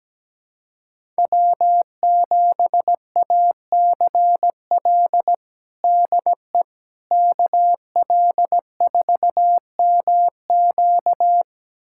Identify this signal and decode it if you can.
Morse code, heard as a keyed tone, about 17 wpm, 700 Hz. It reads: W7ACL DE KL4MQ
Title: USA Callsign DE Callsign - 5 Characters - 17wpm